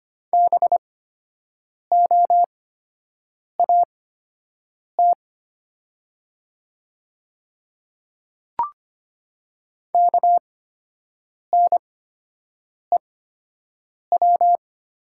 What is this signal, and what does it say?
Morse code, heard as a keyed tone, about 25 wpm, spaced as for 7 wpm, 700 Hz. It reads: BOAT KNEW
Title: 8x Character Spacing - Top 500 Words - 25wpm